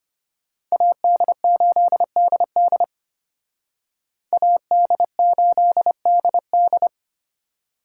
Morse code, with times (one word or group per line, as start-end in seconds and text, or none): 0.7-2.8 AD8DD
4.3-6.9 AD8DD